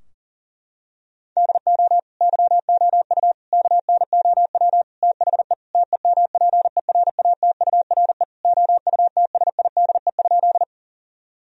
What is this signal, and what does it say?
Morse code, heard as a keyed tone, about 40 wpm, 700 Hz. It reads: DO YOU KNOW THE TEMPERATURE OUTSIDE?
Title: Sentences from Top 1000 Words - Review Part 4 - 40wpm